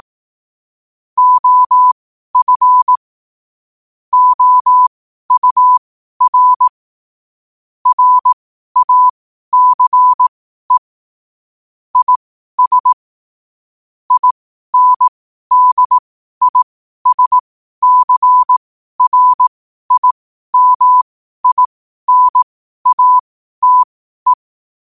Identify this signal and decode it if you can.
Morse code, heard as a keyed tone, about 18 wpm, spaced as for 12 wpm, 1000 Hz. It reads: OF OUR RACE IS INDISCRIMINATE
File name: qotdfiles / 12wpm